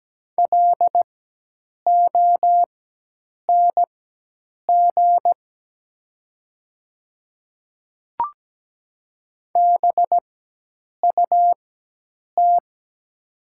Morse code, heard as a keyed tone, about 17 wpm, spaced as for 8 wpm, 700 Hz. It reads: LONG BUT
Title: 4x Character Spacing - Top 100 Words - 17wpm